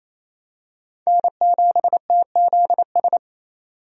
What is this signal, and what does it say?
Morse code, heard as a keyed tone, about 28 wpm, 700 Hz. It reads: N7TZS
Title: USA Call Signs - 5 Characters - 28wpm